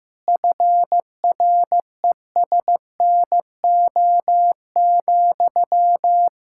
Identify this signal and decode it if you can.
Morse code, heard as a keyed tone, about 15 wpm, 700 Hz. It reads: FRESNO,
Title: Practice QSO - Exchange 1 - Beginning - 15wpm